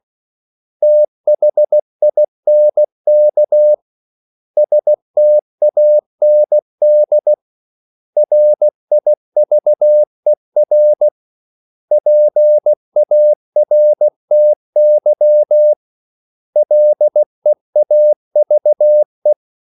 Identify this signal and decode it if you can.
Morse code, heard as a keyed tone, about 16 wpm, 600 Hz. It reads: THINK STAND RIVER PARTY LEAVE
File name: CW / Practice / 500MostCommonEnglishWords16